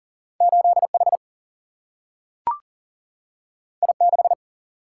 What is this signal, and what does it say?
Morse code, heard as a keyed tone, about 40 wpm, 700 Hz. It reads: 8 H I 6
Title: Single-Letter Single-Number 40wpm